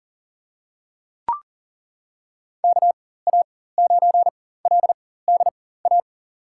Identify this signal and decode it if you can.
Morse code, heard as a keyed tone, about 40 wpm, spaced as for 19 wpm, 700 Hz. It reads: KA9LDA
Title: USA Call Signs - 4x Character Spacing - 40wpm